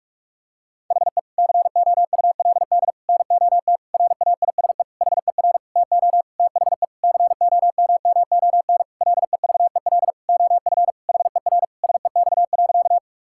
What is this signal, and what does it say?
Morse code, heard as a keyed tone, about 42 wpm, 700 Hz. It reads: HE COULD NOT RAISE HER TO THE COMMON LEVEL OF HER SEX.
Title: Sentences from Top 1600 Words 45wpm